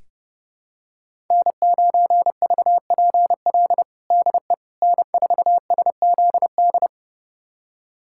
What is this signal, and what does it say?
Morse code, heard as a keyed tone, about 30 wpm, 700 Hz. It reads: N9VPL DE N4SZD
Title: USA Callsign DE Callsign - 5 Characters - 30wpm